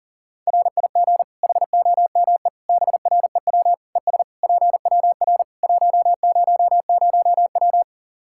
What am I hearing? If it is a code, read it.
Morse code, heard as a keyed tone, about 37 wpm, 700 Hz. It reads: RIG HOME BREW ES PWR 100W
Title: Practice QSO - Exchange 2 - Info - 40wpm